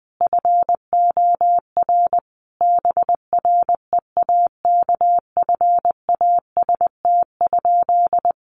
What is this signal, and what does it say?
Morse code, heard as a keyed tone, about 20 wpm, 700 Hz. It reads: FOR BREAKFAST?
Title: Sentences from Top 1500 Words 20wpm